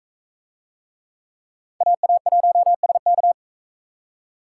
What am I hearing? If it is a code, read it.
Morse code, heard as a keyed tone, about 40 wpm, 700 Hz. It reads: AA1SK